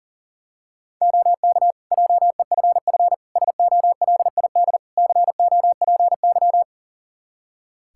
Morse code, heard as a keyed tone, about 40 wpm, 700 Hz. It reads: OK JEFF SOLID COPY
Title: K1USN SST QSO Practice 40wpm